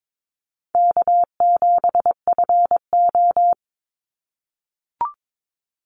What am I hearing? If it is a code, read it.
Morse code, heard as a keyed tone, about 22 wpm, 700 Hz. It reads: K7FO
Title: NA SOTA Chaser Callsigns 22wpm